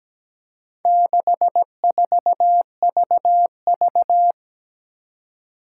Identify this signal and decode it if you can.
Morse code, heard as a keyed tone, about 17 wpm, 700 Hz. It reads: 64VV
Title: Four Challenging Letter-Number Combinations 17wpm